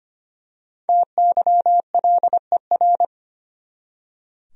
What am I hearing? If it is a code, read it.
Morse code, heard as a keyed tone, about 25 wpm, 700 Hz. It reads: TYLER
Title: USA Common Names - Max 5 Letters - 25wpm